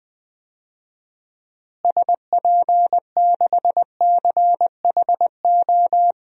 Morse code, heard as a keyed tone, about 20 wpm, 700 Hz. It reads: SP6CHO
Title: DXCC-100 Call Signs - 6 Characters - Repeated 3x - 20wpm